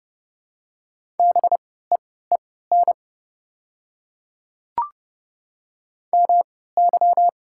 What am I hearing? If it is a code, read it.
Morse code, heard as a keyed tone, about 31 wpm, spaced as for 17 wpm, 700 Hz. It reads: BEEN MY
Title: 3x Character Spacing - Top 100 Words - 30wpm